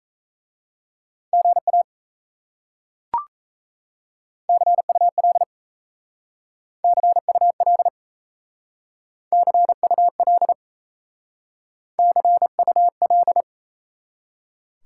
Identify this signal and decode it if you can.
Morse code, heard as a keyed tone, about 38 wpm, 700 Hz. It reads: GA CUL CUL CUL CUL